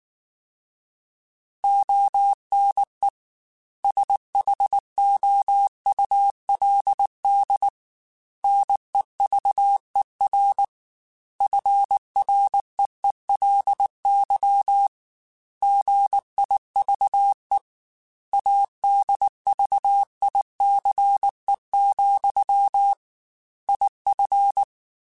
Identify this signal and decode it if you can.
Morse code, heard as a keyed tone, about 19 wpm, 800 Hz. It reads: ONE SHOULD NEVER FREELY GIVE ADVICE, IF
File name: Quotes / Quotes-mp3s / Advice @18